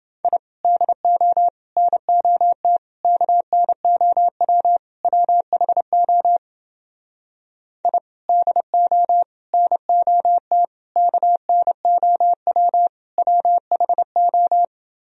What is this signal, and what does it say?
Morse code, heard as a keyed tone, about 30 wpm, 700 Hz. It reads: I DO NOT KNOW WHO I DO NOT KNOW WHO